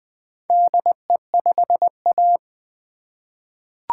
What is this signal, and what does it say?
Morse code, heard as a keyed tone, about 20 wpm, 700 Hz. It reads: DE5A
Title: Four Letter-Number Combinations - 20wpm